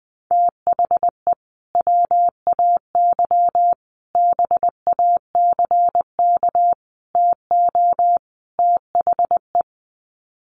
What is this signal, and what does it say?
Morse code, heard as a keyed tone, about 20 wpm, 700 Hz. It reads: THE WAY BACK TO THE
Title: Sets of 5 Words Encore2 20wpm